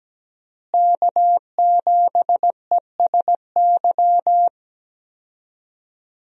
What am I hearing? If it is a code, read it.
Morse code, heard as a keyed tone, about 17 wpm, 700 Hz. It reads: K7ESY
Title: USA Call Signs - 5 Characters - Repeated 2x - 17wpm